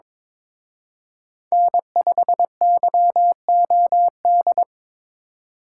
Morse code, heard as a keyed tone, about 27 wpm, 700 Hz. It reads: N5YOD
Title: Speed Racing - USA 1x3 Call Signs 22wpm